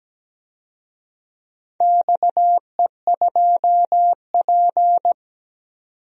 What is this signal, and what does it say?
Morse code, heard as a keyed tone, about 17 wpm, 700 Hz. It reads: XE2P
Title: DXCC-100 Call Signs - 4 Characters - 17wpm